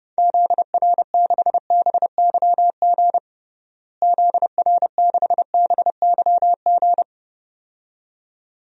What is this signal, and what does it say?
Morse code, heard as a keyed tone, about 30 wpm, 700 Hz. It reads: ZR6BYG ZR6BYG
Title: DXCC-100 Call Signs - 6 Characters - Repeated 2x - 30wpm